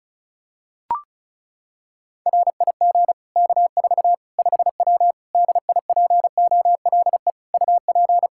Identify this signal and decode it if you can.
Morse code, heard as a keyed tone, about 34 wpm, 700 Hz. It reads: RIG K4 5W DIPOLE UP
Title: Practice QSO - Exchange 2 - Info - 35wpm